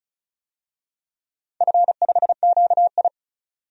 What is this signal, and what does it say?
Morse code, heard as a keyed tone, about 36 wpm, 700 Hz. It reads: F5QI
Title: DXCC-100 Call Signs - 4 Characters - 35wpm